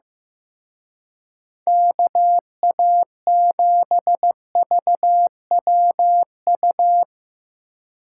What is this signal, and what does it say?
Morse code, heard as a keyed tone, about 17 wpm, 700 Hz. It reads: KA7VWU
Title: Speed Racing - USA 2x3 Call Signs 15wpm